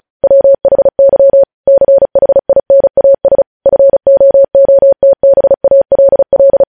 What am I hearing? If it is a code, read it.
Morse code, heard as a keyed tone, about 36 wpm, 550 Hz. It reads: WHY CHINAS FOOTBALL
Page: News headlines in Morse code at 35 wpm